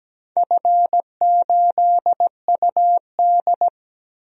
Morse code, heard as a keyed tone, about 17 wpm, 700 Hz. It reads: F8UD
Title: DXCC-100 Call Signs - 4 Characters - 17wpm